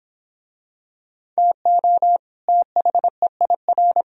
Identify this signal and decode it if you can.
Morse code, heard as a keyed tone, about 24 wpm, 700 Hz. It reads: TO THEIR